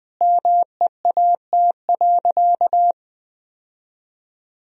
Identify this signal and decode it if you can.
Morse code, heard as a keyed tone, about 21 wpm, 700 Hz. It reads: MEAT.
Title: Sentences from Top 1000 Words - Review Part 3 - 20wpm